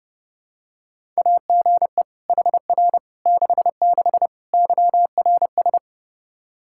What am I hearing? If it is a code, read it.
Morse code, heard as a keyed tone, about 32 wpm, 700 Hz. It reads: AGE HR 66 YRS
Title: Practice QSO - Exchange 3 - More Info - 30wpm